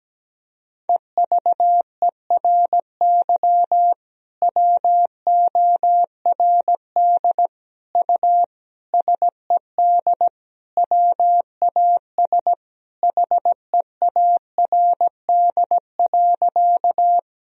Morse code, heard as a keyed tone, about 17 wpm, 700 Hz. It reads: EVERY WORD U SED WAS HEARD.
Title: Sentences from Top 400 Words and Abbreviations 17wpm